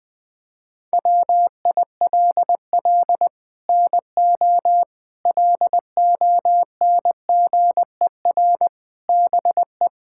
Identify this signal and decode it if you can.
Morse code, heard as a keyed tone, about 21 wpm, 700 Hz. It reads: WILL NO LONGER BE
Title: Sets of 6 Words - 20wpm